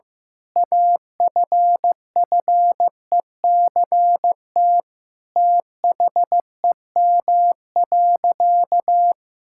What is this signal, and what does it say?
Morse code, heard as a keyed tone, about 15 wpm, 700 Hz. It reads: AFFECT THEM.
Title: Sentences from Top 1000 Words - Review Part 6 - 15wpm